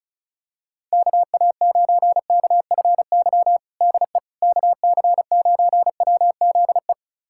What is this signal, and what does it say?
Morse code, heard as a keyed tone, about 34 wpm, 700 Hz. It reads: KA9KFY DE KC9WZE
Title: USA Callsign DE Callsign - 6 Characters - 35wpm